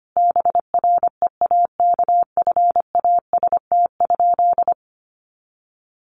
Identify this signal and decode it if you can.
Morse code, heard as a keyed tone, about 25 wpm, 700 Hz. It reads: BREAKFAST?
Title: Sentences from Top 1500 Words 25wpm